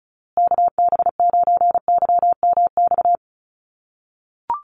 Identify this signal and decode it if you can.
Morse code, heard as a keyed tone, about 34 wpm, 700 Hz. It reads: KB9YMX
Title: USA Call Signs Repeated 3x 35wpm